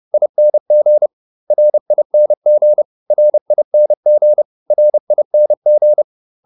Morse code, heard as a keyed tone, about 30 wpm, 600 Hz. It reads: ING RING RING RING